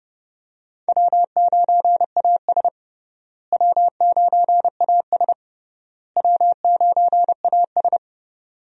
Morse code, heard as a keyed tone, about 32 wpm, 700 Hz. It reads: W9AS W9AS W9AS
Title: USA Call Signs - 4 Characters - Repeated 3x - 30wpm